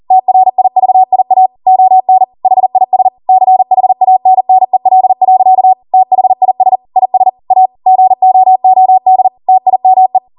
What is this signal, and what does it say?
Morse code, heard as a keyed tone, about 40 wpm, 750 Hz. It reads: TRIVIA ON HIS CHANNEL. THIS IS A GOOD TIME